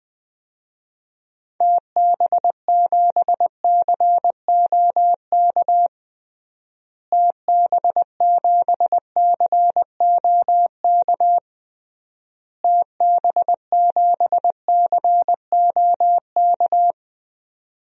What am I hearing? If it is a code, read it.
Morse code, heard as a keyed tone, about 21 wpm, 700 Hz. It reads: TB7COK TB7COK TB7COK
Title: DXCC-100 Call Signs - 6 Characters - Repeated 3x - 20wpm